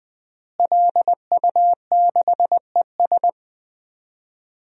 Morse code, heard as a keyed tone, about 20 wpm, 700 Hz. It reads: LU6ES
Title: DXCC-100 Call Signs - 5 Characters - Repeated 2x - 20wpm